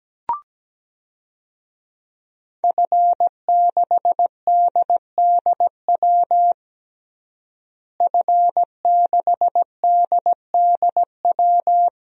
Morse code, heard as a keyed tone, about 17 wpm, 700 Hz. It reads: F6DDW F6DDW
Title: DXCC-100 Call Signs - 5 Characters - Repeated 2x - 17wpm